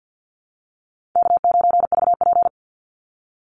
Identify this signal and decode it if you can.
Morse code, heard as a keyed tone, about 50 wpm, 700 Hz. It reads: K9VP
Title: Easy Call Signs Repeated 2x - 50wpm